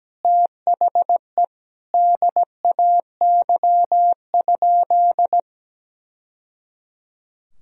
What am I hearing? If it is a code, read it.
Morse code, heard as a keyed tone, about 17 wpm, 700 Hz. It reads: THE DAY?
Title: Sentences from Top 1000 Words - Review Part 4 - 17wpm